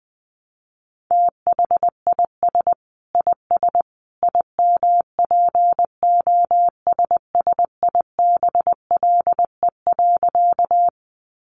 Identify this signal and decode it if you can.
Morse code, heard as a keyed tone, about 20 wpm, 700 Hz. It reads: THIS IS IMPOSSIBLE.
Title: A Sound Of Thunder 20wpm